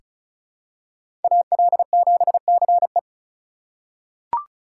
Morse code, heard as a keyed tone, about 36 wpm, 700 Hz. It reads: AL7CE
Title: USA Call Signs - 5 Characters - 4x Character Spacing - 35wpm